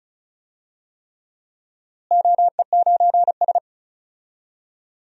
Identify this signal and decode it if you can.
Morse code, heard as a keyed tone, about 34 wpm, 700 Hz. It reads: OE9S
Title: DXCC-100 Call Signs - 4 Characters - Repeated 3x - 35wpm